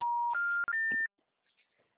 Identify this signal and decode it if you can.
Special information tone.